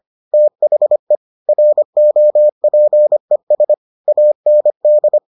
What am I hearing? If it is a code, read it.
Morse code, heard as a keyed tone, about 25 wpm, 600 Hz. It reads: THE ROPES AND